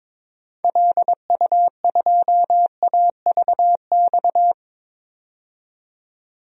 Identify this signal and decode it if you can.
Morse code, heard as a keyed tone, about 22 wpm, 700 Hz. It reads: LU2AVX
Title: DXCC-100 Call Signs - 6 Characters - 22wpm